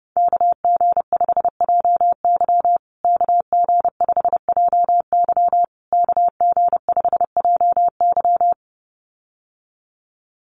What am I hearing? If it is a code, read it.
Morse code, heard as a keyed tone, about 30 wpm, 700 Hz. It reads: KG5JY KG5JY KG5JY
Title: USA Call Signs Repeated 3x 30wpm